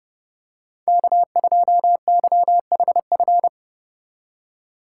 Morse code, heard as a keyed tone, about 29 wpm, 700 Hz. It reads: K2YHF
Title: USA Call Signs - 3x Character Spacing - 30wpm